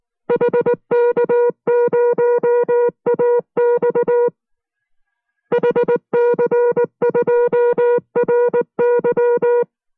Morse code, heard as a keyed tone, about 19 wpm, 480 Hz. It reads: HK0AX HC2RY